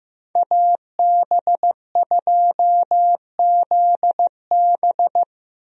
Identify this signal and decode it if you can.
Morse code, heard as a keyed tone, about 15 wpm, 700 Hz. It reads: AB2ZB
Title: USA Call Signs - 5 Characters - Repeated 3x - 15wpm